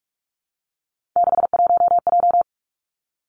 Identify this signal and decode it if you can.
Morse code, heard as a keyed tone, about 46 wpm, 700 Hz. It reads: 61J
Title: Three Challenging Letter-Number Combinations 45wpm